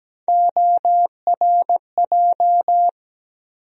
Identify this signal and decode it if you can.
Morse code, heard as a keyed tone, about 17 wpm, 700 Hz. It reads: ORJ
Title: USA Call Sign Postfixes - 3 Letters 17wpm